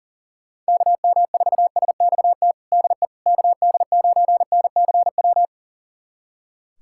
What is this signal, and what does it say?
Morse code, heard as a keyed tone, about 40 wpm, 700 Hz. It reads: KM4SXT DE KD9NCW
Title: USA Callsign DE Callsign - 6 Characters - 40wpm